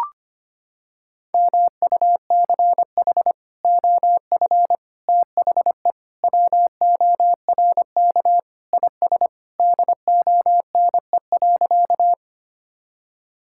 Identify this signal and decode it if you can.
Morse code, heard as a keyed tone, about 25 wpm, 700 Hz. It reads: MUCH OF THE WORK IS DONE.
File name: practice / 25wpm / Sentences with 3-Word Phrases - 25wpm